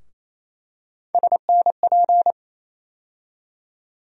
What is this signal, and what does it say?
Morse code, heard as a keyed tone, about 28 wpm, 700 Hz. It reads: SNP
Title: USA Call Sign Postfixes - 3 Letters 28wpm